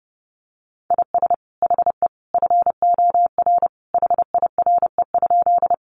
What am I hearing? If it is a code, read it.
Morse code, heard as a keyed tone, about 30 wpm, 700 Hz. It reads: IS HE FOR HIRE?
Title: Sentences from Top 1400 Words 30wpm